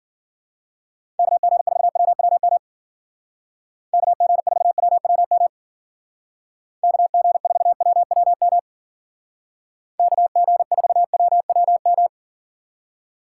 Morse code, heard as a keyed tone, about 49 wpm, 700 Hz. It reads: KG4WWM KG4WWM KG4WWM KG4WWM
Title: Speed Racing - USA 2x3 Call Signs 40wpm